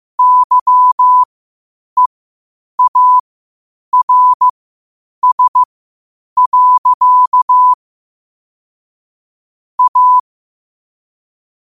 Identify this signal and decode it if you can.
Morse code, heard as a keyed tone, about 15 wpm, spaced as for 8 wpm, 1000 Hz. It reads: YEARS. A
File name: qotdfiles / 8wpm